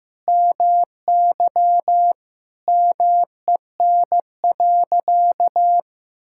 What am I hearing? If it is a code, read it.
Morse code, heard as a keyed tone, about 16 wpm, 700 Hz. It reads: MY MEN.
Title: Sentences from Top 1000 Words - Review Part 6 - 15wpm